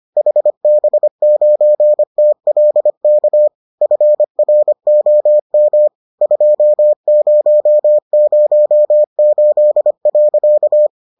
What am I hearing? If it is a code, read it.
Morse code, heard as a keyed tone, about 25 wpm, 600 Hz. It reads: HB9TLK FROM 2008.